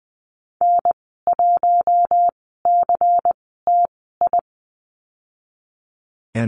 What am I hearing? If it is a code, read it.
Morse code, heard as a keyed tone, about 19 wpm, spaced as for 14 wpm, 700 Hz. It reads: N1CTI
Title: 154 Lesson 25g - Letter p - Call Signs 20wpm